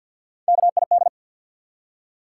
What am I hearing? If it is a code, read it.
Morse code, heard as a keyed tone, about 53 wpm, 700 Hz. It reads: KID
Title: Three Letter Words - 50wpm